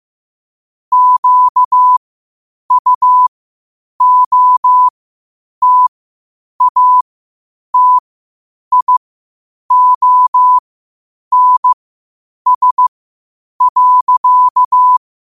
Morse code, heard as a keyed tone, about 15 wpm, spaced as for 8 wpm, 1000 Hz. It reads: QUOTATIONS.
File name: qotdfiles / 8wpm